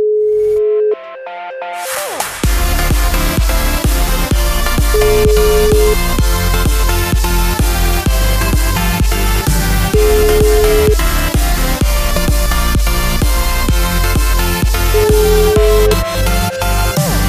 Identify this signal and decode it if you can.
Ringback tone (Europe, 425 Hz), with noise.